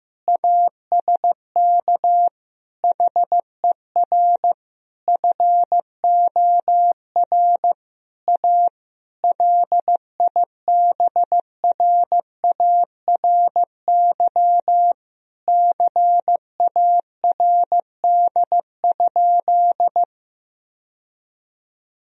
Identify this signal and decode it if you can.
Morse code, heard as a keyed tone, about 16 wpm, 700 Hz. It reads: ASK HER FOR A LIBRARY CARD?
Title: Sentences from Top 1000 Words - Review Part 4 - 15wpm